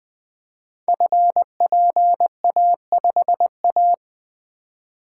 Morse code, heard as a keyed tone, about 20 wpm, 700 Hz. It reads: FPA5A
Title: Code Groups - 5 Letter:Numbers - 20wpm